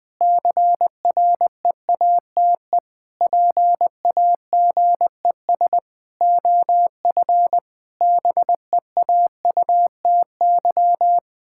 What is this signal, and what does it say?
Morse code, heard as a keyed tone, about 21 wpm, 700 Hz. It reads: CREATE PAGES OF BEAUTY
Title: Pangram Challenge v2 - 20wpm